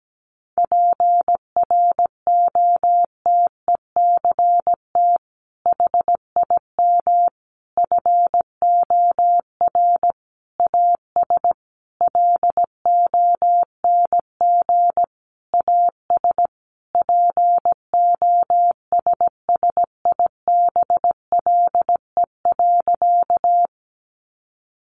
Morse code, heard as a keyed tone, about 17 wpm, 700 Hz. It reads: PROTECT HIM FOR AS LONG AS POSSIBLE.
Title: Sentences from Top 900 Words 17wpm